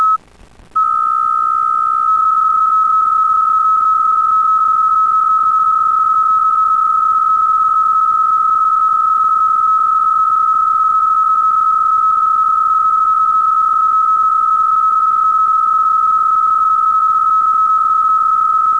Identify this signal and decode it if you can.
PSK63, about 62.5 baud.